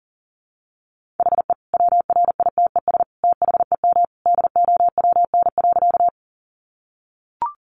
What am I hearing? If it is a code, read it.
Morse code, heard as a keyed tone, about 37 wpm, 700 Hz. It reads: HE WRITES THEM DOWN.
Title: Additional Easy Phrases 40wpm